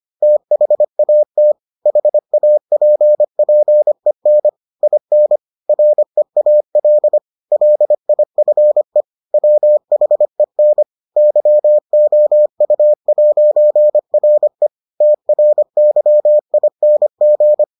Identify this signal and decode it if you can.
Morse code, heard as a keyed tone, about 25 wpm, 600 Hz. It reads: THAT HAPPEN IN REAL LIFE WHEN YOU'RE TRYING